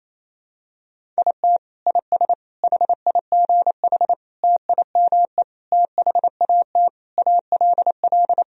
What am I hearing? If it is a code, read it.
Morse code, heard as a keyed tone, about 28 wpm, 700 Hz. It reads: IT IS HIGH TIME THAT ALL
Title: Sentences from Top 1000 Words - Review Part 1 - 28wpm